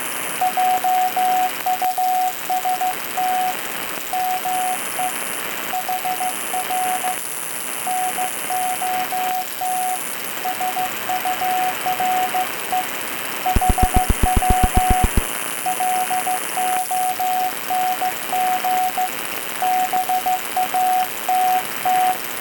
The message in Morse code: JUST ME HR NOT SURE HW LONG BATT